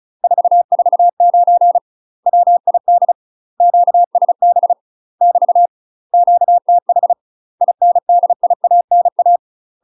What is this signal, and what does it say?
Morse code, heard as a keyed tone, about 36 wpm, 700 Hz. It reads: 449 WID QSB <BT> QTH INDIANA
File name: UK MASTER QSOs TRAINING / 35WPM / UK MASTER QSOs 0014